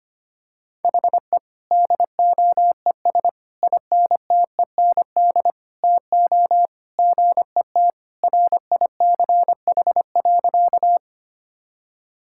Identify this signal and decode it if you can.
Morse code, heard as a keyed tone, about 25 wpm, 700 Hz. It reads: HE DOES INTEND TO GET RICH.
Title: Sentences from Top 1600 Words 25wpm